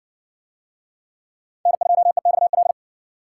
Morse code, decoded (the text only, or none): N2EXL